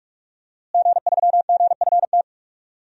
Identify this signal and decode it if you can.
Morse code, heard as a keyed tone, about 46 wpm, 700 Hz. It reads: G3GFT
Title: DXCC-100 Call Signs - 5 Characters - Repeated 2x - 45wpm